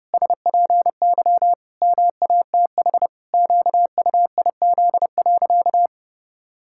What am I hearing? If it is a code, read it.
Morse code, heard as a keyed tone, about 30 wpm, 700 Hz. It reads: SPY MATH QUIZ.
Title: Pangram Challenge v2 - 30wpm